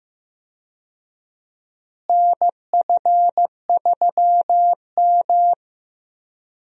Morse code, heard as a keyed tone, about 15 wpm, 700 Hz. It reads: NF3M
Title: USA Call Signs - 4 Characters - Repeated 2x - 15wpm